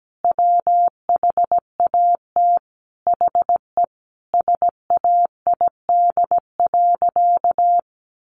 Morse code, heard as a keyed tone, about 17 wpm, 700 Hz. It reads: WHAT HE SAID.
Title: Sentences from Top 400 Words 17wpm